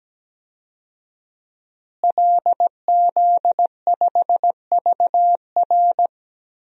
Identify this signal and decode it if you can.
Morse code, heard as a keyed tone, about 17 wpm, 700 Hz. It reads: LZ5VR
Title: DXCC-100 Call Signs - 5 Characters - Repeated 2x - 17wpm